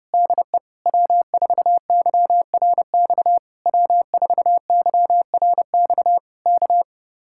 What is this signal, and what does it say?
Morse code, heard as a keyed tone, about 28 wpm, 700 Hz. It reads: DE W4YRX W4YRX K
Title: Speed-Racing Common QSO Phrases - 30wpm